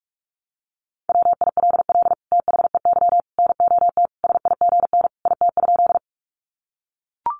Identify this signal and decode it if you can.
Morse code, heard as a keyed tone, about 44 wpm, 700 Hz. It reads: WILL THEY NOT SIGN IT?
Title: Sentences from Top 1000 Words 45wpm